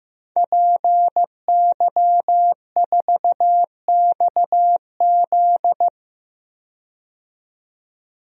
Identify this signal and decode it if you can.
Morse code, heard as a keyed tone, about 15 wpm, 700 Hz. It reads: PY4XZ
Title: DXCC-100 Call Signs - 5 Characters - 15wpm